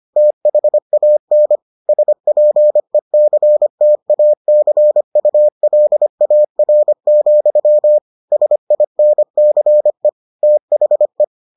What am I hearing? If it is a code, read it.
Morse code, heard as a keyed tone, about 25 wpm, 600 Hz. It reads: THAN SPECTACULAR, SINCE THE